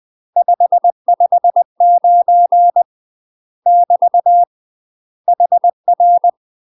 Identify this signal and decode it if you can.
Morse code, heard as a keyed tone, about 20 wpm, 700 Hz. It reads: 559 <BT> HR
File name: UK MASTER QSOs TRAINING / 20WPM / UK MASTER QSOs 0007